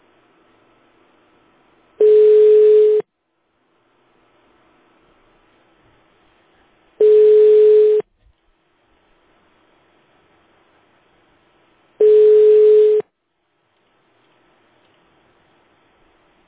Ringback tone (Europe, 425 Hz).